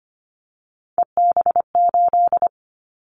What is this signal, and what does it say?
Morse code, heard as a keyed tone, about 25 wpm, 700 Hz. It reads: EB8
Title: Callsign Prefixes for Top 100 DXCC - 25wpm